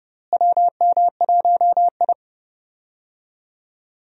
Morse code, heard as a keyed tone, about 31 wpm, 700 Hz. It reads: WM1I
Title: USA Call Signs - 4 Characters - Repeated 3x - 30wpm